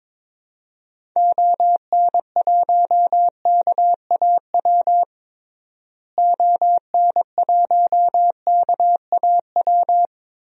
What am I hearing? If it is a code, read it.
Morse code, heard as a keyed tone, about 22 wpm, 700 Hz. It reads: ON1KAW ON1KAW
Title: DXCC-100 Call Signs - 6 Characters - Repeated 3x - 22wpm